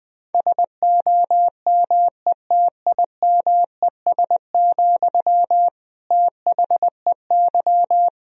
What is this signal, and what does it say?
Morse code, heard as a keyed tone, about 21 wpm, 700 Hz. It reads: SOMETIMES, THEY
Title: Sentences from Top 1000 Words - Review Part 4 - 20wpm